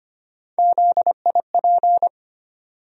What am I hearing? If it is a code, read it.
Morse code, heard as a keyed tone, about 25 wpm, 700 Hz. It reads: ZIP
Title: USA Call Sign Postfixes - 3 Letters 25wpm